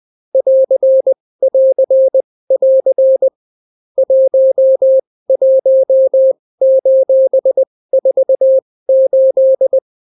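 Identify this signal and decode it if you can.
Morse code, heard as a keyed tone, about 20 wpm, spaced as for 16 wpm, 530 Hz. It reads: <AR><AR><AR> 11:48
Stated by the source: Daily Morse weather and top stories at 16 wpm, clean without QRN.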